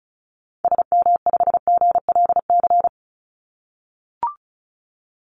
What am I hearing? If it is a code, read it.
Morse code, heard as a keyed tone, about 34 wpm, 700 Hz. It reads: SM5GLC
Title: Call Signs - 35wpm